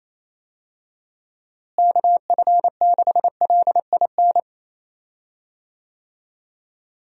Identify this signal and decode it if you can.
Morse code, heard as a keyed tone, about 28 wpm, 700 Hz. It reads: KF6LIN
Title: USA Call Signs - 6 Characters - Repeated 2x - 28wpm